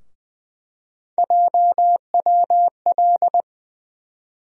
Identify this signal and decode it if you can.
Morse code, heard as a keyed tone, about 21 wpm, 700 Hz. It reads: JWL
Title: USA Call Sign Postfixes - 3 Letters 20wpm